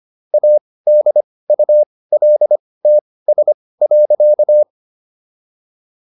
Morse code, heard as a keyed tone, about 25 wpm, spaced as for 18 wpm, 600 Hz. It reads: ADULTS.